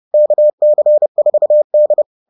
Morse code, heard as a keyed tone, about 30 wpm, 600 Hz. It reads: KC4D